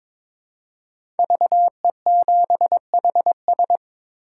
Morse code, heard as a keyed tone, about 22 wpm, 700 Hz. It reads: VE7HS
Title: K1USN SST QSO Practice - 22wpm